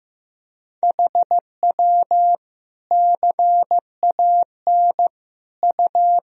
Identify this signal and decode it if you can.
Morse code, heard as a keyed tone, about 15 wpm, 700 Hz. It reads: HW CAN U
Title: Sentences from Top 100 Words and Abbreviations 15wpm